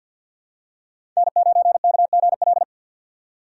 Morse code, heard as a keyed tone, about 46 wpm, 700 Hz. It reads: N9KGL
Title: USA Call Signs - 2x Character Spacing - 50wpm